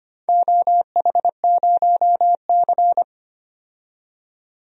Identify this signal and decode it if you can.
Morse code, heard as a keyed tone, about 25 wpm, 700 Hz. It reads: OH0C
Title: DXCC-100 Call Signs - 4 Characters - 25wpm